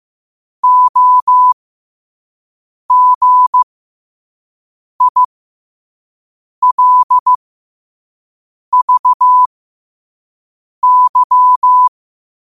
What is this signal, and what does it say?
Morse code, heard as a keyed tone, about 15 wpm, spaced as for 5 wpm, 1000 Hz. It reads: OGILVY
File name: qotdfiles / 5wpm